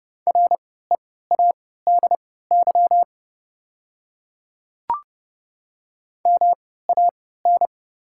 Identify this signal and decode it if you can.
Morse code, heard as a keyed tone, about 30 wpm, spaced as for 17 wpm, 690 Hz. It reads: READY MAN
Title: 3x Character Spacing - Top 500 Words - 30wpm